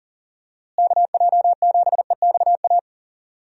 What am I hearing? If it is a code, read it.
Morse code, heard as a keyed tone, about 40 wpm, 700 Hz. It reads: KJ7EXA
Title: USA Call Signs - 6 Characters - Repeated 2x - 40wpm